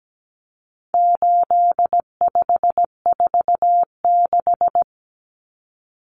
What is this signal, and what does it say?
Morse code, heard as a keyed tone, about 17 wpm, 700 Hz. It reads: 8546
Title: Common Cut Numbers - 4-Digit 17wpm